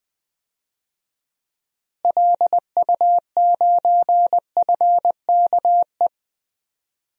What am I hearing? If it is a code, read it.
Morse code, heard as a keyed tone, about 20 wpm, 700 Hz. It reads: LU9FKE
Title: DXCC-100 Call Signs - 6 Characters - Repeated 3x - 20wpm